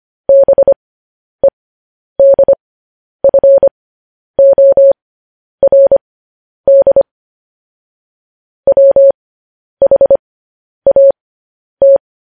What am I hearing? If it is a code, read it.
Morse code, heard as a keyed tone, about 25 wpm, spaced as for 10 wpm, 550 Hz. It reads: BEDFORD WHAT
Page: News Headlines in Morse Code at 10 WPM